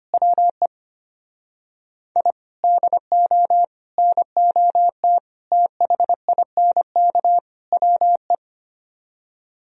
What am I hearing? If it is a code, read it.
Morse code, heard as a keyed tone, about 32 wpm, 700 Hz. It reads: WE I DO NOT THINK WE